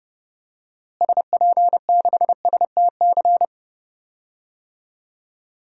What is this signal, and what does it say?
Morse code, heard as a keyed tone, about 32 wpm, 700 Hz. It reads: SP6STC
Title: DXCC-100 Call Signs - 6 Characters - Repeated 3x - 30wpm